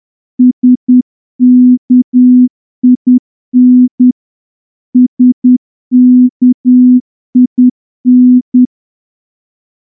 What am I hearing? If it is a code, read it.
Morse code, heard as a keyed tone, about 10 wpm, 250 Hz. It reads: SKIN SKIN
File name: morse-training-10WPM